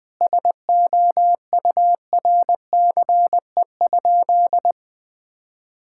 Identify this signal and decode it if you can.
Morse code, heard as a keyed tone, about 20 wpm, 700 Hz. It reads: SOURCE?
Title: Sentences from Top 1000 Words - Review Part 5 - 20wpm